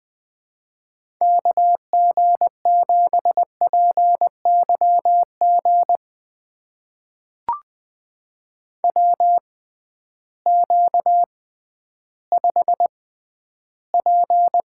Morse code, heard as a keyed tone, about 20 wpm, spaced as for 12 wpm, 700 Hz. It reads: KG7PYG WQ5P
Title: USA Call Signs - 6x Character Spacing - 20wpm